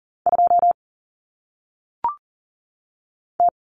Morse code, heard as a keyed tone, about 39 wpm, 700 Hz. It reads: 2 T
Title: Common Cut Numbers - 1-Digit 40wpm